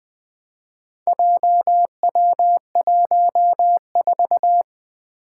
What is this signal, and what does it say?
Morse code, heard as a keyed tone, about 20 wpm, 700 Hz. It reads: JW14
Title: Four Challenging Letter-Number Combinations 20wpm